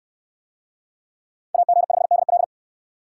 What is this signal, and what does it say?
Morse code, heard as a keyed tone, about 68 wpm, 700 Hz. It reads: AG5DL